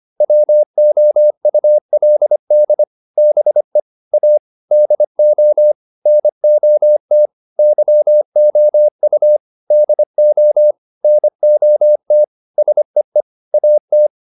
Morse code, heard as a keyed tone, about 25 wpm, 600 Hz. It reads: WOULD BE A DO NOT YOU DO NOT SEE AT
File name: WD204a 25